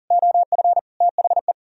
Morse code, heard as a keyed tone, about 52 wpm, 700 Hz. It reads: OF THE